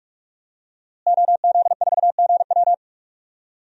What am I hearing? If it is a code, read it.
Morse code, heard as a keyed tone, about 41 wpm, 700 Hz. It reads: OZ4GW
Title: DXCC-100 Call Signs - 5 Characters - Repeated 2x - 45wpm